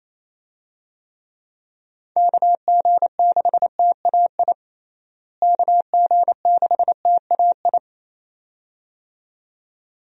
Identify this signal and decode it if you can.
Morse code, heard as a keyed tone, about 28 wpm, 700 Hz. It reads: KG6TAI KG6TAI
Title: USA Call Signs - 6 Characters - Repeated 2x - 28wpm